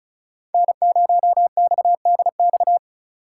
Morse code, heard as a keyed tone, about 46 wpm, 700 Hz. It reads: N0XDX